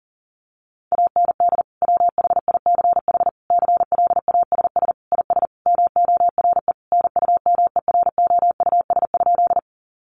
Morse code, heard as a keyed tone, about 40 wpm, 700 Hz. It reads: AND WHICH CLASS IS MORE NUMEROUS?
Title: Sentences from Top 1400 Words 40wpm